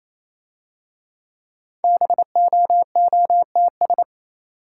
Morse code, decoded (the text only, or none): BOOTS